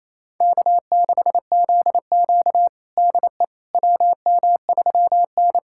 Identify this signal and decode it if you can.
Morse code, heard as a keyed tone, about 27 wpm, 700 Hz. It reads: K6ZQ DE WM3N
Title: USA Callsign DE Callsign - 4 Characters - 28wpm